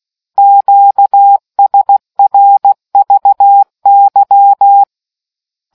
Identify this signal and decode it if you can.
Morse code, heard as a keyed tone, about 16 wpm, 800 Hz. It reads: QSRVY